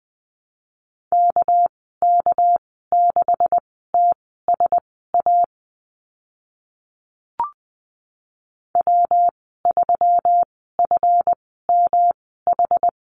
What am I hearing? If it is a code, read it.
Morse code, heard as a keyed tone, about 20 wpm, spaced as for 14 wpm, 700 Hz. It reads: KK6TSA W3FMH
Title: 245 Lesson 39g - Letter z - Call Signs 20wpm